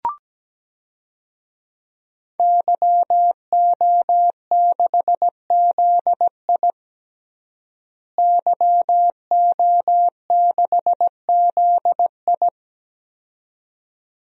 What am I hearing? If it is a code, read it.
Morse code, heard as a keyed tone, about 17 wpm, 700 Hz. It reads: YO6ZI YO6ZI
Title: DXCC-100 Call Signs - 5 Characters - Repeated 2x - 17wpm